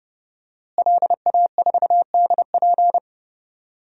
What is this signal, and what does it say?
Morse code, heard as a keyed tone, about 31 wpm, 700 Hz. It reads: LA4DP
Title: DXCC-100 Call Signs - 5 Characters - Repeated 2x - 30wpm